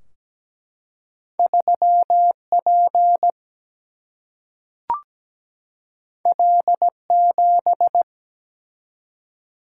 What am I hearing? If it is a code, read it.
Morse code, heard as a keyed tone, about 17 wpm, 700 Hz. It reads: 3 P L 7
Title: Single-Letter Single-Number 17wpm